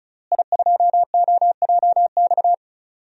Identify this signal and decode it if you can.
Morse code, heard as a keyed tone, about 36 wpm, 690 Hz. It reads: I2OJX
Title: DXCC-100 Call Signs - 5 Characters - Repeated 3x - 35wpm